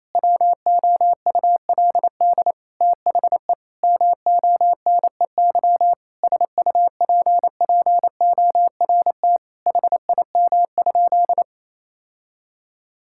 Morse code, heard as a keyed tone, about 28 wpm, 700 Hz. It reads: WOULD THE MONEY SUPPORT HIM?
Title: Sentences from Top 1000 Words - Review Part 5 - 28wpm